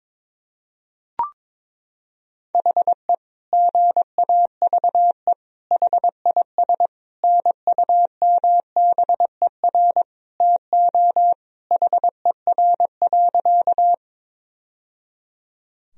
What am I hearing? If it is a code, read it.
Morse code, heard as a keyed tone, about 21 wpm, 700 Hz. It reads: HE GAVE HIS NUMBER TO HER.
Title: Sentences from Top 1000 Words - Review Part 1 - 22wpm